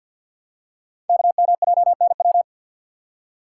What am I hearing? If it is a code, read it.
Morse code, heard as a keyed tone, about 48 wpm, 700 Hz. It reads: KMJNW
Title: Code Groups - 5 Letters - 50wpm